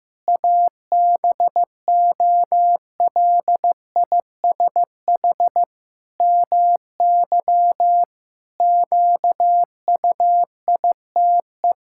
Morse code, heard as a keyed tone, about 15 wpm, 700 Hz. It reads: ABOLISH MY QUITE
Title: Pangram Challenge v2 - 15wpm